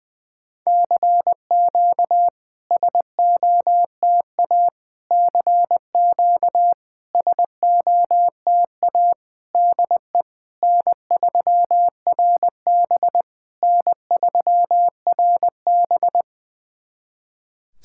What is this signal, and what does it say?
Morse code, heard as a keyed tone, about 20 wpm, 700 Hz. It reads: CQ SOTA CQ SOTA DE N3RB N3RB
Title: SOTA QSO Practice 20wpm